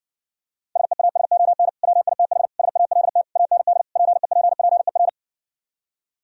Morse code, heard as a keyed tone, about 59 wpm, 700 Hz. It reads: SEASON WITH SALT AND PEPPER